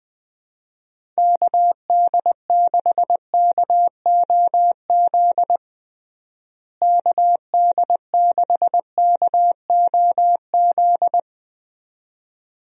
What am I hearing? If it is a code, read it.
Morse code, heard as a keyed tone, about 20 wpm, 700 Hz. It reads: KD6KOZ KD6KOZ
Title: USA Call Signs - 6 Characters - Repeated 2x - 20wpm